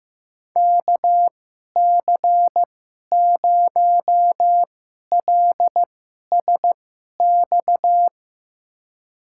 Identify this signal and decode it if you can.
Morse code, heard as a keyed tone, about 15 wpm, spaced as for 11 wpm, 700 Hz. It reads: KC0LSX
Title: USA Call Signs - 2x Character Spacing - 15wpm